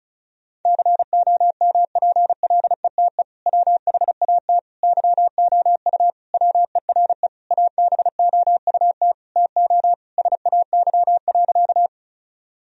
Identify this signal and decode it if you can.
Morse code, heard as a keyed tone, about 34 wpm, 700 Hz. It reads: COMPLETE WHAT YOU WERE ABOUT TO SAY.
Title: Sentences from Top 1000 Words - Review Part 2 - 35wpm